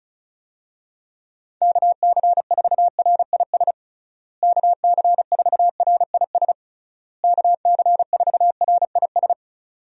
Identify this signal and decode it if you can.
Morse code, heard as a keyed tone, about 36 wpm, 700 Hz. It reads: KC4RIS KC4RIS KC4RIS
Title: USA Call Signs - 6 Characters - Repeated 3x - 35wpm